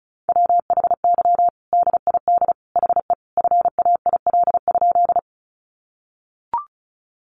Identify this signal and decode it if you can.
Morse code, heard as a keyed tone, about 34 wpm, 700 Hz. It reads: WHY DID HE FAIL?
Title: Sentences from Top 1100 Words 35wpm